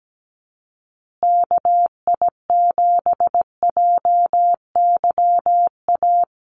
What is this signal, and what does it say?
Morse code, heard as a keyed tone, about 17 wpm, 700 Hz. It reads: KI7JYA
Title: USA Call Signs Repeated 3x 17wpm